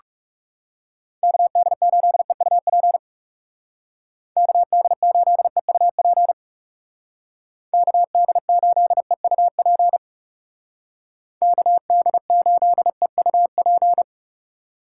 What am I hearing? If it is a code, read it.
Morse code, heard as a keyed tone, about 40 wpm, 700 Hz. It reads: KD8EUP KD8EUP KD8EUP KD8EUP